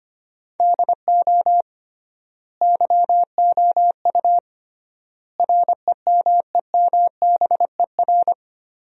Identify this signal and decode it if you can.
Morse code, heard as a keyed tone, about 25 wpm, 700 Hz. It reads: DO YOU REMEMBER
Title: 3x Word-Spacing - Sentences from Top 400 Words - 25wpm